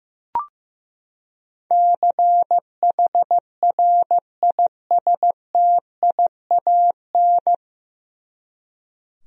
Morse code, heard as a keyed tone, about 15 wpm, 700 Hz. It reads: CHRISTIAN
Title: USA Common Names - 15wpm